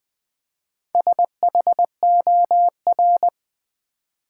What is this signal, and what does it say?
Morse code, heard as a keyed tone, about 20 wpm, 700 Hz. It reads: SHOR
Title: Build A Word - 4 Letters At A Time - 8-Letter Words - 20wpm